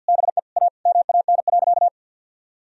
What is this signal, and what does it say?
Morse code, heard as a keyed tone, about 50 wpm, 700 Hz. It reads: BE A MAN.
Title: Sentences from Top 1000 Words - Review Part 4 - 50wpm